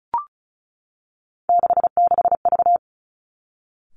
Morse code, heard as a keyed tone, about 36 wpm, 700 Hz. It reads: 66V
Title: Three Challenging Letter-Number Combinations 35wpm